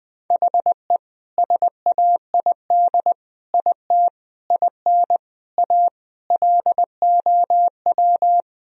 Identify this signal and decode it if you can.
Morse code, heard as a keyed tone, about 21 wpm, 700 Hz. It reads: HE SAID IT IN A LOW
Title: Sentences from Top 1000 Words - Review Part 1 - 20wpm